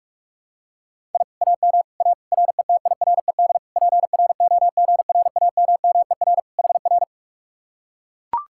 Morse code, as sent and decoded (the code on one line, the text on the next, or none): .. / .- -- / .- / .-. - .. .-. -.. / .--. .-. --- --. .-. .- -- -- . .-. / .... .-.
I AM A RETIRED PROGRAMMER HR